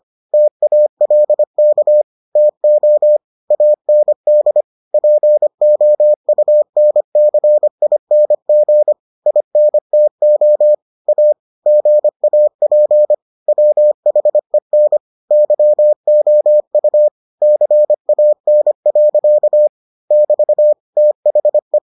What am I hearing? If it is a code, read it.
Morse code, heard as a keyed tone, about 25 wpm, 600 Hz. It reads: TALK TO AND POUNCING INTO A GAP WHEN YOU CAN. <BT> THE